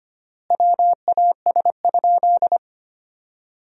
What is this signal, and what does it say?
Morse code, heard as a keyed tone, about 25 wpm, 700 Hz. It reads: WAS?
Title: 3x Word-Spacing - Sentences from Top 200 Words - 25wpm